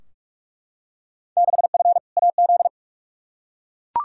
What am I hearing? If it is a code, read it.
Morse code, heard as a keyed tone, about 43 wpm, 700 Hz. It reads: 6F AZ
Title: Field Day Exchanges Encore - 45wpm